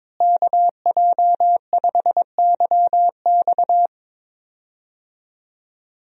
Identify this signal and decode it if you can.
Morse code, heard as a keyed tone, about 22 wpm, 700 Hz. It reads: KJ5YX
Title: USA Call Signs - 5 Characters - Repeated 2x - 22wpm